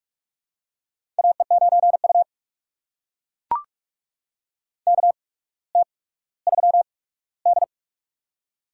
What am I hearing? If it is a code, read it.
Morse code, heard as a keyed tone, about 42 wpm, spaced as for 19 wpm, 700 Hz. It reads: AE9U KT3D
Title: USA Call Signs - 4 Characters - 8x Character Spacing - 45wpm